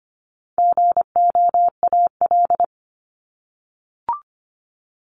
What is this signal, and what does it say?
Morse code, heard as a keyed tone, about 25 wpm, 700 Hz. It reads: GOAL
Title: Top 751-1000 Words 25wpm